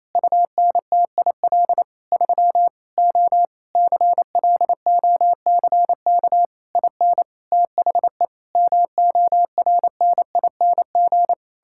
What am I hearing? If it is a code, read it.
Morse code, heard as a keyed tone, about 28 wpm, 700 Hz. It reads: UNTIL 3 O CLOCK IN THE MORNING
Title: Pangram Challenge v2 - 28wpm